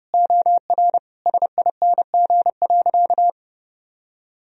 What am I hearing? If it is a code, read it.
Morse code, heard as a keyed tone, about 30 wpm, 700 Hz. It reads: OR SING.
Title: Sentences from Top 1000 Words - Review Part 2 - 30wpm